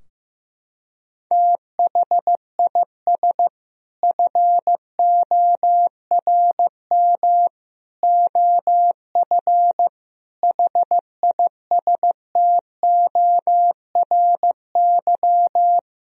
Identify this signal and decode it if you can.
Morse code, heard as a keyed tone, about 15 wpm, 700 Hz. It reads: THIS FORM OF HISTORY
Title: Sentences from Top 1000 Words - Review Part 4 - 15wpm